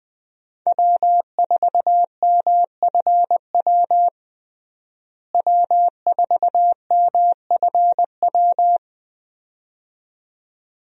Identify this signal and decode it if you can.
Morse code, heard as a keyed tone, about 21 wpm, 700 Hz. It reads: W4MFW W4MFW
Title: USA Call Signs - 5 Characters - Repeated 2x - 20wpm